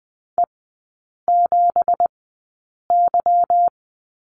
Morse code, heard as a keyed tone, about 20 wpm, spaced as for 8 wpm, 690 Hz. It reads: E7Y
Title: 166 Lesson 27e - Number 7 - Three Characters 20wpm